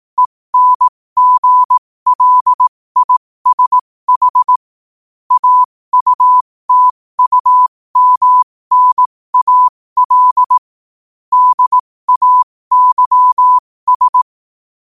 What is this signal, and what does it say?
Morse code, heard as a keyed tone, about 18 wpm, spaced as for 15 wpm, 1000 Hz. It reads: ENGLISH AUTUMNAL DAYS
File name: qotdfiles / 15wpm